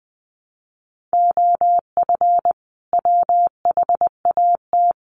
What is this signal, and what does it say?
Morse code, heard as a keyed tone, about 20 wpm, 700 Hz. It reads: OF WHAT
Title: Sentences from Top 600 Words 20wpm